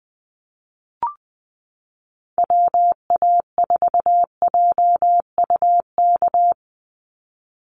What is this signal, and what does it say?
Morse code, heard as a keyed tone, about 20 wpm, 700 Hz. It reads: WA4JUK
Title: Call Signs Repeated 3x - 20wpm